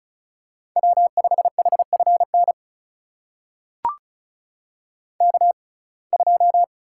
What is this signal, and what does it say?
Morse code, heard as a keyed tone, about 36 wpm, spaced as for 25 wpm, 700 Hz. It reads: W5HFN K2
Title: USA Call Signs - 5 Characters - 6x Character Spacing - 35wpm